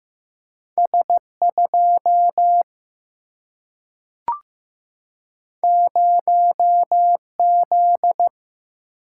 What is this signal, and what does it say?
Morse code, heard as a keyed tone, about 15 wpm, 700 Hz. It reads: S 2 0 Z
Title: Single-Letter Single-Number 15wpm